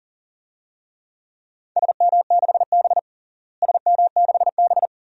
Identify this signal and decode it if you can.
Morse code, heard as a keyed tone, about 40 wpm, 700 Hz. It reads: SM6B SM6B
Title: DXCC-100 Call Signs - 4 Characters - Repeated 2x - 40wpm